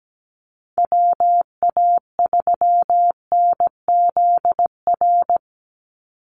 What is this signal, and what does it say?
Morse code, heard as a keyed tone, about 17 wpm, 700 Hz. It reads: WA3NZR
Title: Call Signs - 17wpm